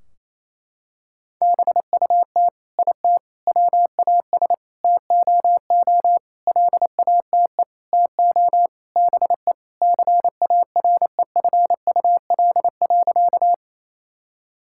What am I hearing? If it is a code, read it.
Morse code, heard as a keyed tone, about 28 wpm, 700 Hz. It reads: BUT IT WAS TOO LATE TO BE CAREFUL.
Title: Sentences from Top 1600 Words 28wpm